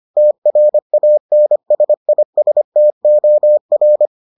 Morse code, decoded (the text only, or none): TRANSISTOR